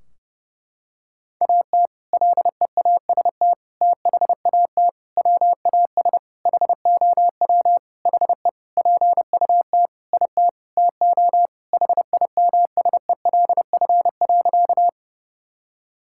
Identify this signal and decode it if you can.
Morse code, heard as a keyed tone, about 30 wpm, 700 Hz. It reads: AT LEAST THAT WAS HOW HE PUT IT TO HIMSELF.
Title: Sentences from Top 1600 Words 30wpm